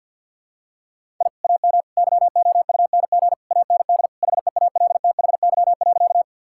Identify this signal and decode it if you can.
Morse code, heard as a keyed tone, about 50 wpm, 700 Hz. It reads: I AM YOUNG AND HEALTHY.
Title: Sentences from Top 1000 Words - Review Part 5 - 50wpm